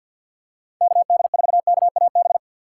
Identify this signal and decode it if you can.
Morse code, heard as a keyed tone, about 53 wpm, 700 Hz. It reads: KD4KAB
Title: USA Call Signs - 4x Character Spacing - 50wpm